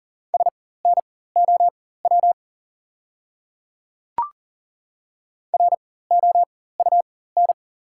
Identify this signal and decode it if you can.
Morse code, heard as a keyed tone, about 40 wpm, spaced as for 19 wpm, 690 Hz. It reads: SNOW ROUN
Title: 4x Character Spacing - Top 500 Words - 40wpm